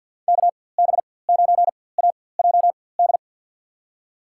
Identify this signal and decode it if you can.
Morse code, heard as a keyed tone, about 52 wpm, spaced as for 23 wpm, 700 Hz. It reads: KB9AJD
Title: USA Call Signs - 4x Character Spacing - 50wpm